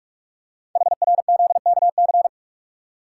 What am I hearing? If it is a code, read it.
Morse code, heard as a keyed tone, about 44 wpm, 690 Hz. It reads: HRZKC
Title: Code Groups - 5 Letters - 45wpm